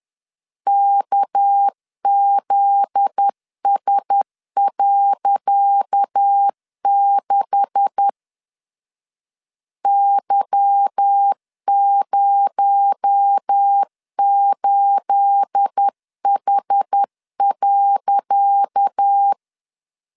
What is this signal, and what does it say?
Morse code, heard as a keyed tone, about 10 wpm, 800 Hz. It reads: KZS.6 Y08H.
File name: LEZIONI / 10 SIGNS / 10C DOT 10wpm 2940